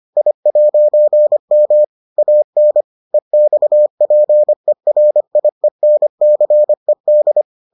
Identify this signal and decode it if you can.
Morse code, heard as a keyed tone, about 25 wpm, 600 Hz. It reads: I'M AN EXPERIENCED